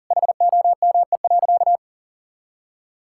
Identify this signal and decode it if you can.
Morse code, heard as a keyed tone, about 37 wpm, 690 Hz. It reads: HOME.
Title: Sentences from Top 1000 Words - Review Part 5 - 40wpm